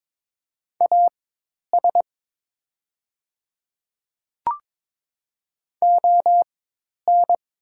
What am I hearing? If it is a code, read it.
Morse code, heard as a keyed tone, about 22 wpm, spaced as for 10 wpm, 700 Hz. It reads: AS ON
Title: 4x Character Spacing - Top 100 Words - 22wpm